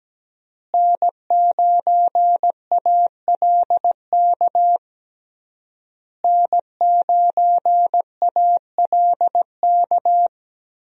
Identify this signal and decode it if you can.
Morse code, heard as a keyed tone, about 17 wpm, 700 Hz. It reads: N9ALK N9ALK
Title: USA Call Signs - 5 Characters - Repeated 3x - 17wpm